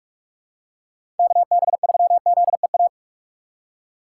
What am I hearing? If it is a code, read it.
Morse code, heard as a keyed tone, about 47 wpm, 700 Hz. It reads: KB3ZEA
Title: USA Call Signs - 2x Character Spacing - 45wpm